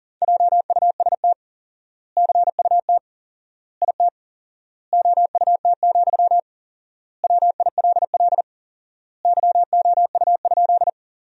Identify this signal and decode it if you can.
Morse code, heard as a keyed tone, about 40 wpm, 700 Hz. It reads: JUST CUT IT OUT, WILL YOU?
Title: 4x Word-Spacing - Sentences from Top 300 Words - 40wpm